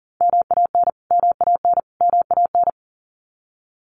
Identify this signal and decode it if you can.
Morse code, heard as a keyed tone, about 40 wpm, 700 Hz. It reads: MAN MAN MAN
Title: Top 500 Words Repeated 40wpm